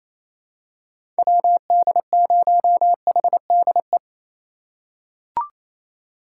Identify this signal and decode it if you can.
Morse code, heard as a keyed tone, about 28 wpm, 700 Hz. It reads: WD0HDE
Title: USA Call Signs - 6 Characters - 2x Character Spacing - 28wpm